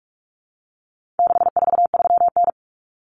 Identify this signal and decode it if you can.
Morse code, heard as a keyed tone, about 44 wpm, 700 Hz. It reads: 643N
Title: Common Cut Numbers - 4-Digit 45wpm